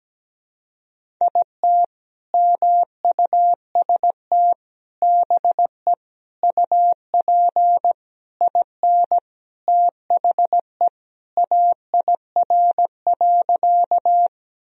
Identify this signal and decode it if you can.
Morse code, heard as a keyed tone, about 17 wpm, 700 Hz. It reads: IT MUST BE UP IN THE AIR.
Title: Sentences from Top 1000 Words - Review Part 1 - 17wpm